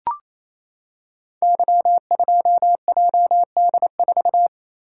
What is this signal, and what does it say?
Morse code, heard as a keyed tone, about 28 wpm, 700 Hz. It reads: Y2JD4
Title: Code Groups - 5 Letter:Numbers - 28wpm